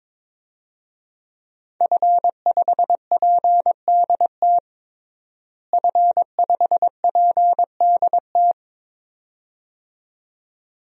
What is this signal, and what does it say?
Morse code, heard as a keyed tone, about 22 wpm, 700 Hz. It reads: F5PDT F5PDT
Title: DXCC-100 Call Signs - 5 Characters - Repeated 2x - 22wpm